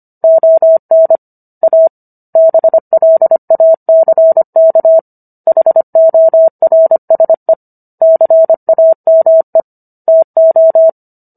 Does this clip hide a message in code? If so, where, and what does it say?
Yes: Morse code from 0.2 to 10.9 s, heard as a keyed tone, about 25 wpm, 650 Hz. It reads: ON A BLACK HORSE CAME TO